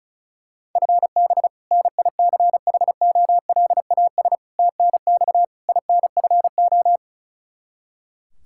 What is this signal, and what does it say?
Morse code, heard as a keyed tone, about 34 wpm, 700 Hz. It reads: FB NICHOLAS TNX INFO
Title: K1USN SST QSO Practice 35wpm